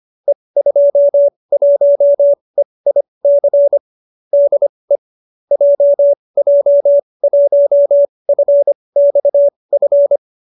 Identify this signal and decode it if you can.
Morse code, heard as a keyed tone, about 25 wpm, spaced as for 20 wpm, 570 Hz. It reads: E21EIC DE JJ1FXF
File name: qso208b 20